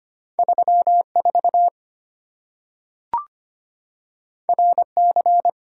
Morse code, heard as a keyed tone, about 25 wpm, 690 Hz. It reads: 34 RC
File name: Two Letter-Number Combinations - 25wpm